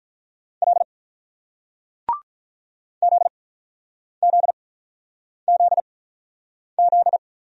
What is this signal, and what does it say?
Morse code, heard as a keyed tone, about 48 wpm, spaced as for 9 wpm, 700 Hz. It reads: L ZZZZ